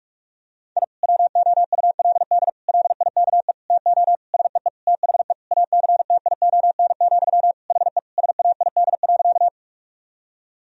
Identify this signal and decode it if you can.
Morse code, heard as a keyed tone, about 46 wpm, 700 Hz. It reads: I WOULD LIKE TO SEE THE ACTION, HE SAID.
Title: Sentences from Top 1000 Words - Review Part 4 - 45wpm